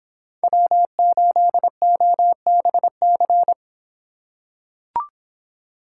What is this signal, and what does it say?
Morse code, heard as a keyed tone, about 24 wpm, 700 Hz. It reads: W8OBC